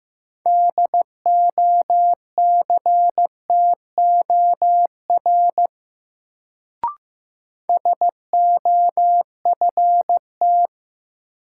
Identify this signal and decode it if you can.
Morse code, heard as a keyed tone, about 15 wpm, 700 Hz. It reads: DOCTOR SOFT
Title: Sentences from Top 1500 Words - New Word Drill Alt - 15wpm